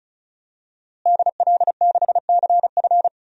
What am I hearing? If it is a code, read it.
Morse code, heard as a keyed tone, about 34 wpm, 700 Hz. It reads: DL6CF
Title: DXCC-100 Call Signs - 5 Characters - Repeated 2x - 35wpm